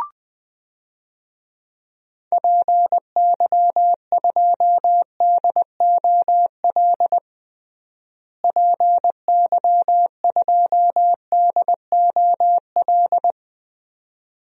Morse code, heard as a keyed tone, about 20 wpm, 700 Hz. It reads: PY2DOL PY2DOL
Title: DXCC-100 Call Signs - 6 Characters - Repeated 2x - 20wpm